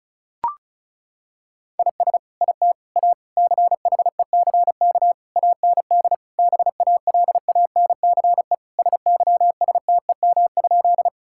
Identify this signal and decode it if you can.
Morse code, heard as a keyed tone, about 36 wpm, 700 Hz. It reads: IS IT A CHECK AND BALANCE SYSTEM?
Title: Sentences from Top 1000 Words - Review Part 6 - 35wpm